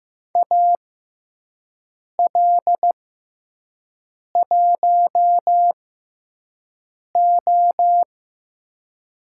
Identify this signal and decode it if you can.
Morse code, heard as a keyed tone, about 15 wpm, spaced as for 5 wpm, 700 Hz. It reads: AL1O
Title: USA Call Signs - 4 Characters - 6x Character Spacing - 15wpm